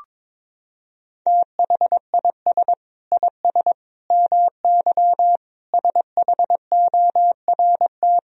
Morse code, heard as a keyed tone, about 21 wpm, 700 Hz. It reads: THIS IS MY SHORT
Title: Sentences from Top 1000 Words - Review Part 4 - 22wpm